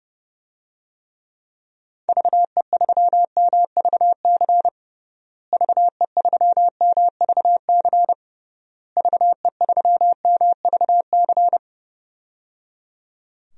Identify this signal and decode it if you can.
Morse code, heard as a keyed tone, about 30 wpm, 700 Hz. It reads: VE3MVC VE3MVC VE3MVC
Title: DXCC-100 Call Signs - 6 Characters - Repeated 3x - 30wpm